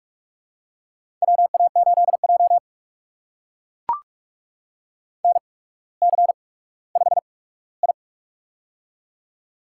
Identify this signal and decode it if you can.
Morse code, heard as a keyed tone, about 44 wpm, spaced as for 19 wpm, 700 Hz. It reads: WA8J NC5I
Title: USA Call Signs - 4 Characters - 8x Character Spacing - 45wpm